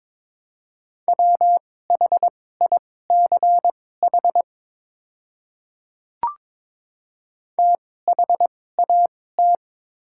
Morse code, heard as a keyed tone, about 22 wpm, spaced as for 16 wpm, 700 Hz. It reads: WHICH THAT
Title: 2x Character Spacing - Top 200 Words - 22wpm